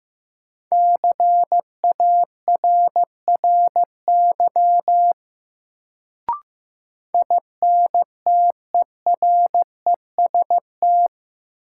Morse code, heard as a keyed tone, about 15 wpm, 700 Hz. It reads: CARRY INTEREST
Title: Sentences from Top 400 Words - New Word Drill - 15wpm